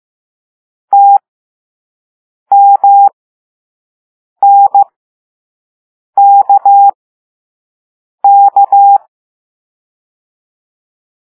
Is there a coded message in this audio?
Yes: Morse code from 0.9 to 9.0 s, heard as a keyed tone, about 15 wpm, spaced as for 5 wpm, 800 Hz. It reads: TMNKK